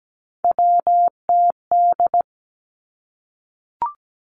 Morse code, heard as a keyed tone, about 17 wpm, 700 Hz. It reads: WTD
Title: Three Challenging Letter-Number Combinations 17wpm